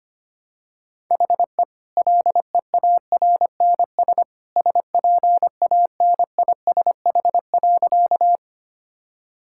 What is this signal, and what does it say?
Morse code, heard as a keyed tone, about 25 wpm, 690 Hz. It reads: HE LEARNS SPANISH.
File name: Easy Sentences with Hints - 25wpm